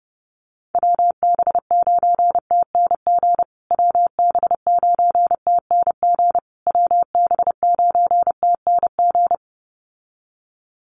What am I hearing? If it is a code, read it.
Morse code, heard as a keyed tone, about 28 wpm, 700 Hz. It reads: WB9TNG WB9TNG WB9TNG
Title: USA Call Signs Repeated 3x 30wpm